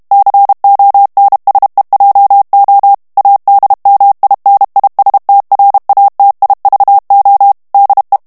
Morse code, heard as a keyed tone, about 32 wpm, 800 Hz. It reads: CONSEJO ADMINISTRATIVO DE
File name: morse mp3 / ESPAGNOL TEXTE EN CLAIR 1 VITESSE 30 MOTS MINUTES